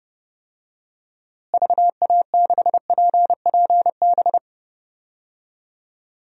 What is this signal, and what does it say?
Morse code, heard as a keyed tone, about 30 wpm, 700 Hz. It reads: VA6PPB
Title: DXCC-100 Call Signs - 6 Characters - Repeated 2x - 30wpm